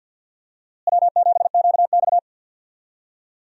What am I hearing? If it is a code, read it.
Morse code, heard as a keyed tone, about 52 wpm, 690 Hz. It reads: W7QX
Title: USA Call Signs - 4 Characters - Repeated 2x - 50wpm